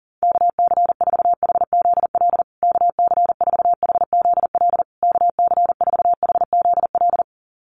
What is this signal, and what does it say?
Morse code, heard as a keyed tone, about 40 wpm, 700 Hz. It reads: KC4HZL KC4HZL KC4HZL
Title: USA Call Signs Repeated 3x 40wpm